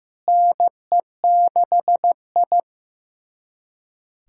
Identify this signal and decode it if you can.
Morse code, heard as a keyed tone, about 16 wpm, 700 Hz. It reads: NE6I
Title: USA Callsign DE Callsign - 4 Characters - 15wpm